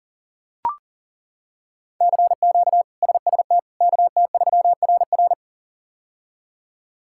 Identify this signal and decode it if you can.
Morse code, heard as a keyed tone, about 40 wpm, 700 Hz. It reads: CQ SST KT3RR
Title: K1USN SST QSO Practice - 40wpm